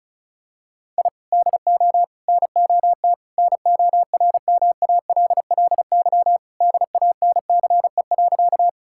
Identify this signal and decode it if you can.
Morse code, heard as a keyed tone, about 36 wpm, 700 Hz. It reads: I DO NOT NORMALLY DANCE.
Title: Sentences from Top 1000 Words - Review Part 6 - 35wpm